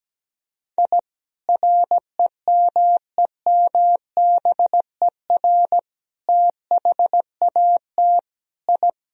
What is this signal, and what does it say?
Morse code, heard as a keyed tone, about 17 wpm, 700 Hz. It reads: I REMEMBER THAT I
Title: Sentences from Top 1000 Words - Review Part 1 - 17wpm